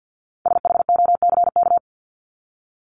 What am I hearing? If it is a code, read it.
Morse code, heard as a keyed tone, about 50 wpm, 700 Hz. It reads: SHOCK
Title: Top 1751-2000 Words 50wpm